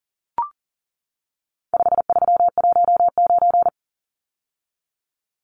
Morse code, heard as a keyed tone, about 40 wpm, 700 Hz. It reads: 5319
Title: Numbers - 4-Digit 40wpm